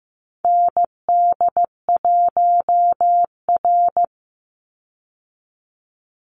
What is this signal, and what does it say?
Morse code, heard as a keyed tone, about 15 wpm, 700 Hz. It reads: ND1R
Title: Call Signs Repeated 3x - 15wpm